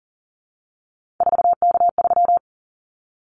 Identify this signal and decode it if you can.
Morse code, heard as a keyed tone, about 40 wpm, 700 Hz. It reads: 4K3
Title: Callsign Prefixes for Top 100 DXCC - 40wpm